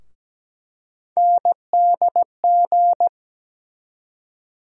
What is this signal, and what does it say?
Morse code, heard as a keyed tone, about 17 wpm, 700 Hz. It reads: NDG
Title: USA Call Sign Postfixes - 3 Letters 17wpm